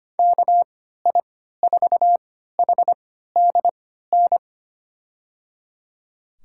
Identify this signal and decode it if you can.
Morse code, heard as a keyed tone, about 25 wpm, spaced as for 14 wpm, 700 Hz. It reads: KI4HDN
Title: USA Call Signs - 3x Character Spacing - 25wpm